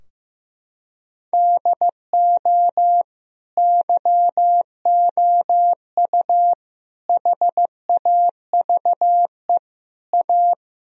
Morse code, heard as a keyed tone, about 15 wpm, 700 Hz. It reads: DO YOU HAVE A
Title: Sentences from Top 1000 Words - Review Part 3 - 15wpm